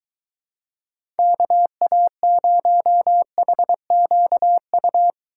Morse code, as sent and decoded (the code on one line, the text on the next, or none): -.- .- ----- .... --.- ..-
KA0HQU